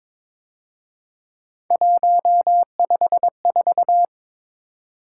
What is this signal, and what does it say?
Morse code, heard as a keyed tone, about 26 wpm, 700 Hz. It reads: 154
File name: Speed-Racing - Three Letter-Number Combinations - 22wpm